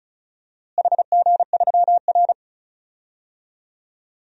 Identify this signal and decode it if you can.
Morse code, heard as a keyed tone, about 36 wpm, 700 Hz. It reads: HG3R
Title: DXCC-100 Call Signs - 4 Characters - Repeated 2x - 35wpm